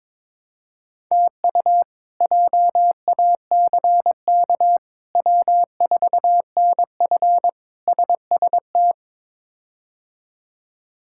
Morse code, heard as a keyed tone, about 22 wpm, 700 Hz. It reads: TU JACK W4NF SST
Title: K1USN SST QSO Practice - 22wpm